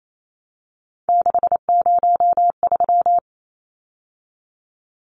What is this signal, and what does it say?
Morse code, heard as a keyed tone, about 28 wpm, 700 Hz. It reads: 603
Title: Numbers - 3-Digit 28wpm